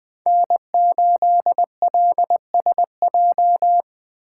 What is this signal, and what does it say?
Morse code, heard as a keyed tone, about 19 wpm, 700 Hz. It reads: N8LSJ